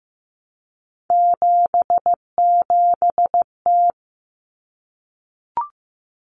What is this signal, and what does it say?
Morse code, heard as a keyed tone, about 15 wpm, 700 Hz. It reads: 77T
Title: Common Cut Numbers - 3-Digit 15wpm